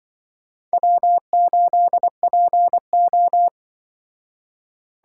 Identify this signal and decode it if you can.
Morse code, heard as a keyed tone, about 25 wpm, 700 Hz. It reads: W8PO
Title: Speed Racing - USA 1x2 Call Signs 20wpm